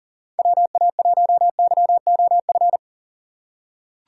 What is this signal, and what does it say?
Morse code, heard as a keyed tone, about 40 wpm, 700 Hz. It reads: WA1YOF
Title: USA Call Signs - 6x Character Spacing - 40wpm